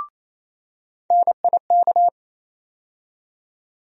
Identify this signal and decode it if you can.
Morse code, heard as a keyed tone, about 28 wpm, 700 Hz. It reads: NIK
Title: USA Call Sign Postfixes - 3 Letters 28wpm